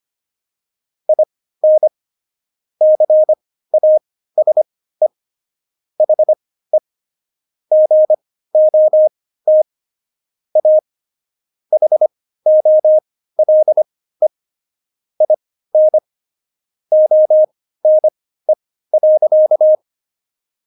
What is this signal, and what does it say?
Morse code, heard as a keyed tone, about 25 wpm, spaced as for 15 wpm, 620 Hz. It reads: IN CASE HE GOT A HOLE IN ONE.
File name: Jokes-25-char-15-eff